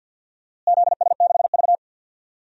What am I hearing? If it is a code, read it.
Morse code, heard as a keyed tone, about 48 wpm, 700 Hz. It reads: ZS6V
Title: DXCC-100 Call Signs - 4 Characters - Repeated 2x - 50wpm